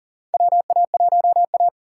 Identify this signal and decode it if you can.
Morse code, heard as a keyed tone, about 40 wpm, 700 Hz. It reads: WA1A
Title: USA Call Signs - 4 Characters - 4x Character Spacing - 40wpm